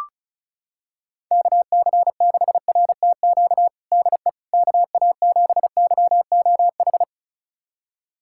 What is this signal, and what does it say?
Morse code, heard as a keyed tone, about 36 wpm, 700 Hz. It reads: KC6RTQ DE KA7YOH
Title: USA Callsign DE Callsign - 6 Characters - 35wpm